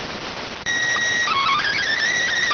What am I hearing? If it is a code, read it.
SSTV (slow-scan television), PD 290.